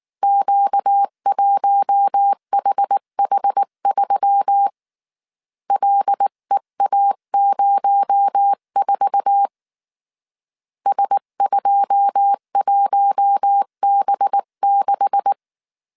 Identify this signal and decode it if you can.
Morse code, heard as a keyed tone, about 19 wpm, 800 Hz. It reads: Q1HH3 LEA04 S21B6
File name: CW / DET / 1 9 1500C S B 18wpm 2551